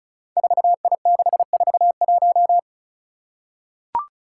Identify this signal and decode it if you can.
Morse code, heard as a keyed tone, about 36 wpm, 700 Hz. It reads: 4I641
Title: Five Challenging Letter-Number Combinations 35wpm